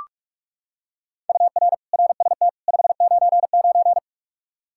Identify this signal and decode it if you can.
Morse code, heard as a keyed tone, about 46 wpm, 700 Hz. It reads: UR RST 599
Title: Practice QSO - Exchange 1 - Beginning - 45wpm